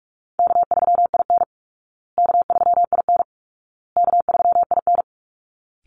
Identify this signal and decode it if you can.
Morse code, heard as a keyed tone, about 44 wpm, 700 Hz. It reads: K3IN K3IN K3IN
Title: Easy Call Signs Repeated 3x - 45wpm